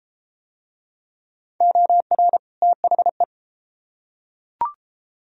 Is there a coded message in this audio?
Yes: Morse code from 1.6 to 3.2 s, heard as a keyed tone, about 32 wpm, 700 Hz. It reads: OR THE